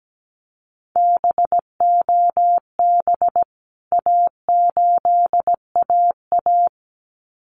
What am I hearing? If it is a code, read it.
Morse code, heard as a keyed tone, about 17 wpm, 700 Hz. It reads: BOB A8AA
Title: CWT Exchange 17wpm